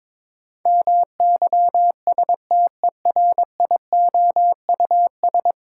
Morse code, heard as a keyed tone, about 22 wpm, 700 Hz. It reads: MYSTERIOUS
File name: practice / 22wpm / 10 Letter Words - First 3 Letters Given - 22wpm